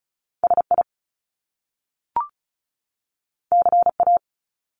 Morse code, heard as a keyed tone, about 36 wpm, 700 Hz. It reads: SI CA
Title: Top Two Letter Combinations in Words 35wpm